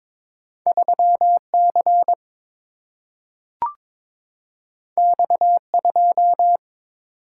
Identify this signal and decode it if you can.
Morse code, heard as a keyed tone, about 22 wpm, 700 Hz. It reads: 3C X2
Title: Single-Letter Single-Number 22wpm